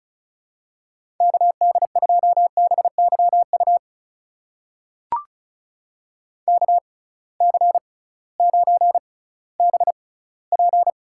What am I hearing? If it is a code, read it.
Morse code, heard as a keyed tone, about 36 wpm, spaced as for 19 wpm, 700 Hz. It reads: KD2BYU KC9BP
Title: USA Call Signs - 6 Characters - 6x Character Spacing - 35wpm